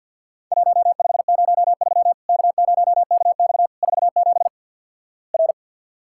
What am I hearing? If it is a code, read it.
Morse code, heard as a keyed tone, about 50 wpm, 700 Hz. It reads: 1503 K0KX 47 R
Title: Worked All Europe DX Contest - QTC Exchange with Repeats - 50wpm